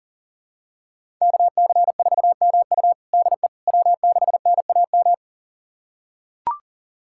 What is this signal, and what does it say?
Morse code, heard as a keyed tone, about 40 wpm, 700 Hz. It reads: KC4MU DE W6NAM
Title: USA Callsign DE Callsign - 5 Characters - 40wpm